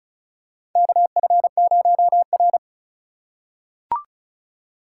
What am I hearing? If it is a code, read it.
Morse code, heard as a keyed tone, about 34 wpm, 700 Hz. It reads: KF0R
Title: USA Call Signs - 4 Characters - 4x Character Spacing - 35wpm